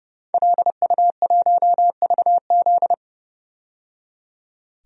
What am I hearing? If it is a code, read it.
Morse code, heard as a keyed tone, about 29 wpm, 700 Hz. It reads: LU1VZ
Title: DXCC-100 Call Signs - 5 Characters - 30wpm